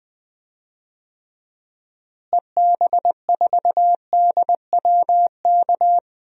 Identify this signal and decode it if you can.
Morse code, heard as a keyed tone, about 20 wpm, 700 Hz. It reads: EB4DWK
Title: DXCC-100 Call Signs - 6 Characters - Repeated 3x - 20wpm